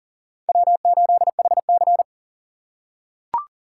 Morse code, heard as a keyed tone, about 40 wpm, 700 Hz. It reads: W8HC
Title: USA Call Signs - 4 Characters - 3x Character Spacing - 40wpm